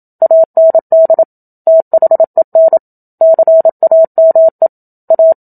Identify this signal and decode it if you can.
Morse code, heard as a keyed tone, about 27 wpm, 650 Hz. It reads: AND THEN CAME A
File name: Book Two / 01-Chapter 1 - 27 WPM